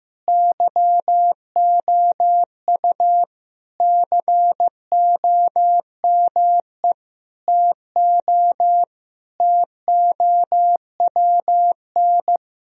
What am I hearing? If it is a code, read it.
Morse code, heard as a keyed tone, about 15 wpm, 700 Hz. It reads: YOU COME TO TOWN
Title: Sentences from Top 1000 Words - Review Part 2 - 15wpm